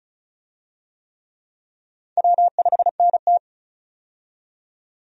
Morse code, heard as a keyed tone, about 34 wpm, 700 Hz. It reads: W5NT
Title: USA Call Signs - 4 Characters - 35wpm